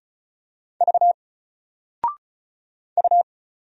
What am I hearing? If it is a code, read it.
Morse code, heard as a keyed tone, about 36 wpm, 700 Hz. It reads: V U
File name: Single-Letter-Number-35wpm